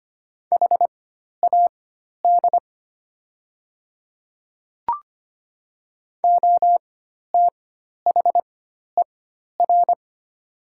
Morse code, heard as a keyed tone, about 25 wpm, spaced as for 12 wpm, 700 Hz. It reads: HAD OTHER
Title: 4x Character Spacing - Top 100 Words - 25wpm